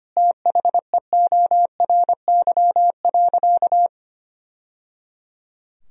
Morse code, heard as a keyed tone, about 25 wpm, 700 Hz. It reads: THEORY.
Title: Sentences from Top 1000 Words - Review Part 3 - 25wpm